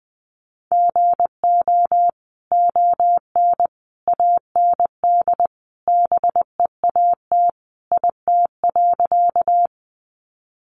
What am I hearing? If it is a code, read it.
Morse code, heard as a keyed tone, about 20 wpm, 700 Hz. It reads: GO ON AND BEAT IT.
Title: Sentences from Top 1300 Words 20wpm